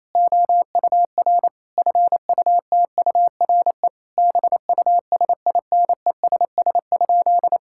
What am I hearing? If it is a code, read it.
Morse code, heard as a keyed tone, about 28 wpm, 700 Hz. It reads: OUR FUTURE BUSINESS?
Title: Sentences from Top 1000 Words - Review Part 6 - 28wpm